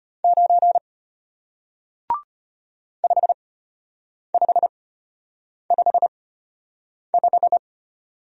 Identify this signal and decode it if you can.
Morse code, heard as a keyed tone, about 34 wpm, spaced as for 8 wpm, 700 Hz. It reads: 9 5555